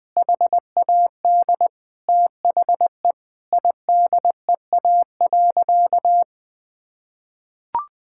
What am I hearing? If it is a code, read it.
Morse code, heard as a keyed tone, about 20 wpm, 700 Hz. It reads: HAD THE IDEA.
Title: Sentences with 2-Word Phrases - Spoken Beginning - 20wpm